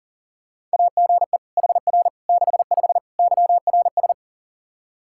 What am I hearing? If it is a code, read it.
Morse code, heard as a keyed tone, about 40 wpm, 700 Hz. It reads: AGE HR 65 YRS
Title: Practice QSO - Exchange 3 - More Info - 40wpm